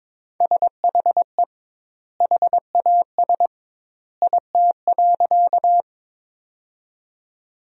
Morse code, heard as a keyed tone, about 21 wpm, 700 Hz. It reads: SHE HAS IT.
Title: 2x Word-Spacing - Sentences from Top 100 Words - 22wpm